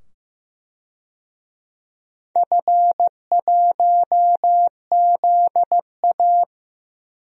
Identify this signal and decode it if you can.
Morse code, heard as a keyed tone, about 15 wpm, 700 Hz. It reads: F1ZA
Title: DXCC-100 Call Signs - 4 Characters - 15wpm